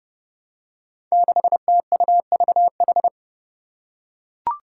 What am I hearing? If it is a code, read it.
Morse code, heard as a keyed tone, about 30 wpm, 700 Hz. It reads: 6TUVH
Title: Five Challenging Letter-Number Combinations 30wpm